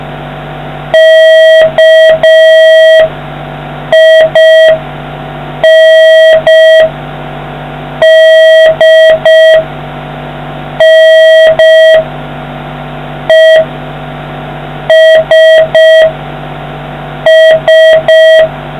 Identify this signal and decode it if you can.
Morse code, heard as a keyed tone, about 5 wpm, spaced as for 4 wpm, 1950 Hz, with noise. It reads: KINDNESS